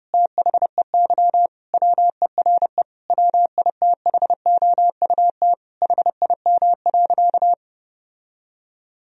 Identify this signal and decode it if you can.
Morse code, heard as a keyed tone, about 30 wpm, 700 Hz. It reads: THEY WERE WITHOUT HIM.
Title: Sentences from Top 1000 Words - Review Part 3 - 30wpm